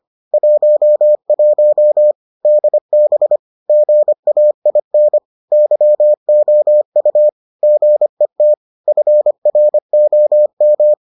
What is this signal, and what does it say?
Morse code, heard as a keyed tone, about 25 wpm, 600 Hz. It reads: 11 DB GAIN YOU GET FROM